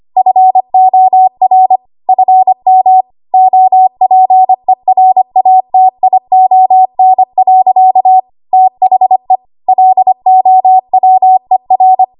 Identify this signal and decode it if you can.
Morse code, heard as a keyed tone, about 25 wpm, 750 Hz. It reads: FOR FM OPERATION. THE LOWER
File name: Archive / 25 WPM / 25WPM